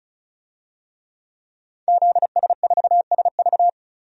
Morse code, heard as a keyed tone, about 34 wpm, 700 Hz. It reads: ZS4SV
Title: DXCC-100 Call Signs - 5 Characters - 35wpm